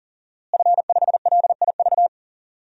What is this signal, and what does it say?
Morse code, heard as a keyed tone, about 37 wpm, 700 Hz. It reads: F5LIV
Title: DXCC-100 Call Signs - 5 Characters - Repeated 3x - 40wpm